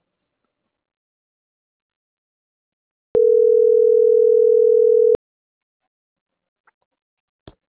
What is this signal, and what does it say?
Ringback tone (North America), 440 + 480 Hz.